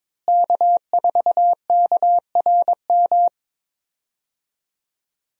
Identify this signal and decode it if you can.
Morse code, heard as a keyed tone, about 21 wpm, 700 Hz. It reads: K4KRM
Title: USA Call Signs - 5 Characters - Repeated 3x - 22wpm